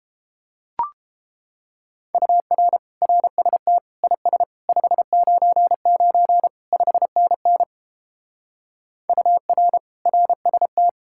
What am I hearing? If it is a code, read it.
Morse code, heard as a keyed tone, about 31 wpm, 700 Hz. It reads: UR RST IS 599 5NN UR RST